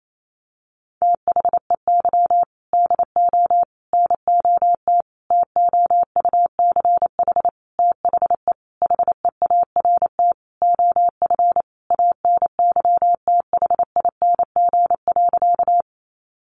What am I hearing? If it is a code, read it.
Morse code, heard as a keyed tone, about 28 wpm, 700 Hz. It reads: THEY DO NOT TOUCH THE HEART OF ANYTHING.
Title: Sentences from Top 1200 Words 28wpm